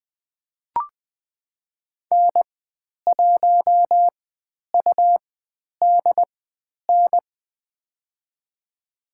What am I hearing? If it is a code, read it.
Morse code, heard as a keyed tone, about 20 wpm, spaced as for 10 wpm, 700 Hz. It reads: N1UDN
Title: 109 Lesson 18g - Letter m - Call Signs 20wpm